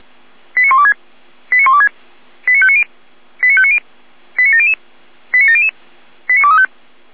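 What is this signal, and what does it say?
Five-tone selective call (ZVEI), heard as quick runs of five tones, tones of about 70 ms. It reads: 89126 89126 89590 89590 89700 89700 89235
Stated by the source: Five-tone ZVEI selective call received by SDR.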